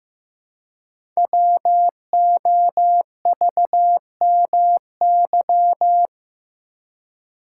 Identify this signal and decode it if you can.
Morse code, heard as a keyed tone, about 15 wpm, 700 Hz. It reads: WOVMY
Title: Code Groups - 5 Letters - 15wpm